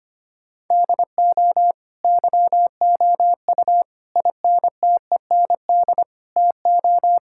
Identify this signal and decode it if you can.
Morse code, heard as a keyed tone, about 25 wpm, 700 Hz. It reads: DO YOU INTEND TO
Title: Sentences from Top 1600 Words 25wpm